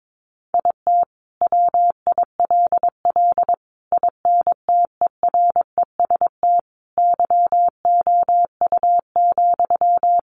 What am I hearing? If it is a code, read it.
Morse code, heard as a keyed tone, about 22 wpm, 700 Hz. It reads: IT WILL INTEREST YOU,
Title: Sentences from Top 400 Words 22wpm